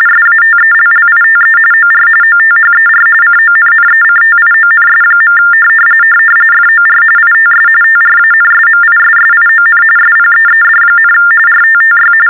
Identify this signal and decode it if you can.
TTY/TDD text telephone tones, about 45.45 baud. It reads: SATELLITE OR OTHER CONNECTIONS. FOR ITS TRANSMISSION SPEED, RTTY
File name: teletype-RTTY-45